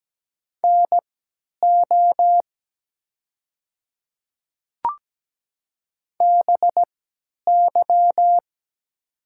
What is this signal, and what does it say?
Morse code, heard as a keyed tone, about 17 wpm, spaced as for 10 wpm, 690 Hz. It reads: NO BY
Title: 3x Character Spacing - Top 100 Words - 17wpm